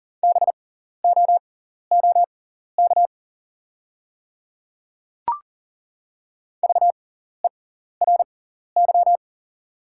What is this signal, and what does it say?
Morse code, heard as a keyed tone, about 38 wpm, spaced as for 14 wpm, 700 Hz. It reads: BOOK VERY
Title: 6x Character Spacing - Top 300 Words - 40wpm